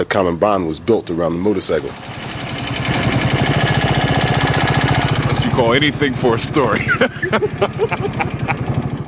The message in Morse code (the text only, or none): none